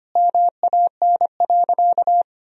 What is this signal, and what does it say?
Morse code, heard as a keyed tone, about 25 wpm, 700 Hz. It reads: MAN.
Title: Sentences from Top 400 Words and Abbreviations 25wpm